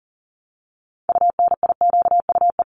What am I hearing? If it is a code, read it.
Morse code, heard as a keyed tone, about 40 wpm, 700 Hz. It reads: UNIQUE
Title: Top 751-1000 Words 40wpm